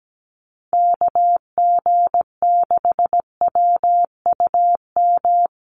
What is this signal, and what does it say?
Morse code, heard as a keyed tone, about 17 wpm, 700 Hz. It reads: KG6WUM
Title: USA Call Signs Repeated 3x 17wpm